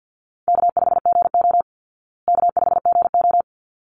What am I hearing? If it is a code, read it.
Morse code, heard as a keyed tone, about 49 wpm, 700 Hz. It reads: K5GO K5GO
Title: Easy Call Signs Repeated 2x - 50wpm